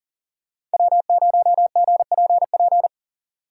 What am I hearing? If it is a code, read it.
Morse code, heard as a keyed tone, about 39 wpm, 700 Hz. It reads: W0GPP
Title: USA Call Signs - 5 Characters - 3x Character Spacing - 40wpm